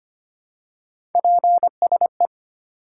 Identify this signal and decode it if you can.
Morse code, heard as a keyed tone, about 25 wpm, 700 Hz. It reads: PSE
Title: Top 90 QSO Elements 25wpm